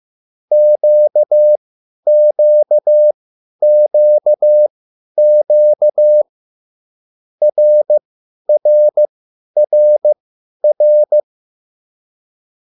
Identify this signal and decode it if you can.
Morse code, heard as a keyed tone, about 15 wpm, spaced as for 10 wpm, 600 Hz. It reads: QQQQ RRRR
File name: QSL / CW lesson